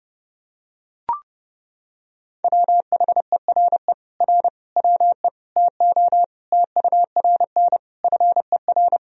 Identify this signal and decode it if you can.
Morse code, heard as a keyed tone, about 32 wpm, 700 Hz. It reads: WHERE R WE TO TURN FER
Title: Sentences from Top 200 Words and Abbreviations 30wpm